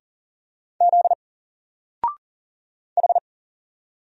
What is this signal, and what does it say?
Morse code, heard as a keyed tone, about 39 wpm, 700 Hz. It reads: Z H
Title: Single Character (Letter:Number:Punctuation:Prosign) 40wpm